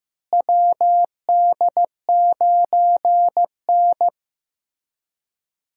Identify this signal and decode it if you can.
Morse code, heard as a keyed tone, about 15 wpm, 700 Hz. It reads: WD9N
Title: USA Call Signs - 4 Characters - Repeated 3x - 15wpm